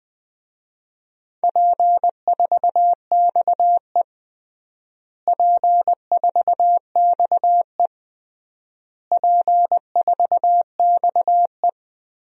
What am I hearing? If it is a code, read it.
Morse code, heard as a keyed tone, about 20 wpm, 700 Hz. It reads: P4XE P4XE P4XE
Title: DXCC-100 Call Signs - 4 Characters - Repeated 3x - 20wpm